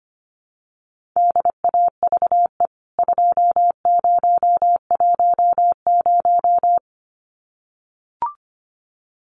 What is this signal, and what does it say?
Morse code, heard as a keyed tone, about 25 wpm, 690 Hz. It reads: DAVE 2010
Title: CWT Exchange 25wpm